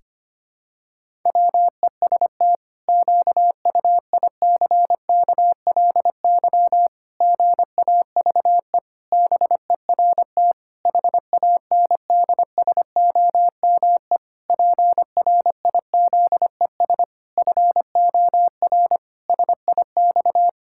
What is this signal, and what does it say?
Morse code, heard as a keyed tone, about 25 wpm, 700 Hz. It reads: WEST QUICKLY GAVE BERT HANDSOME PRIZES FOR SIX
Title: Pangram Challenge v2 - 25wpm